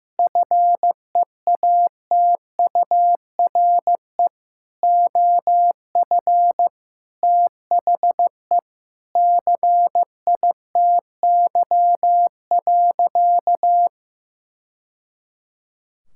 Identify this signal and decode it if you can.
Morse code, heard as a keyed tone, about 15 wpm, 700 Hz. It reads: FEATURE OF THE CITY.
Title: Sentences from Top 1000 Words - Review Part 5 - 15wpm